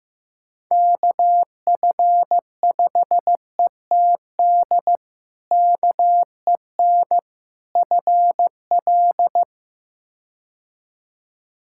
Morse code, heard as a keyed tone, about 15 wpm, 700 Hz. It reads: KF5ETD KEN FL
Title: K1USN SST QSO Practice - 15wpm